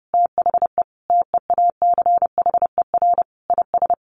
Morse code, heard as a keyed tone, about 30 wpm, 700 Hz. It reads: THE TEACHER IS
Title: Additional Easy Phrases 30wpm